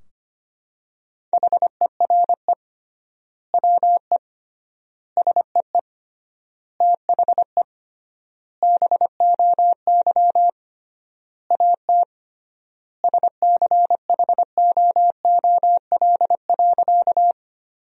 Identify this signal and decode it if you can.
Morse code, heard as a keyed tone, about 25 wpm, 700 Hz. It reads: HERE WE SEE THE BOY AT SCHOOL.
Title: 3x Word-Spacing - Sentences from Top 300 Words - 25wpm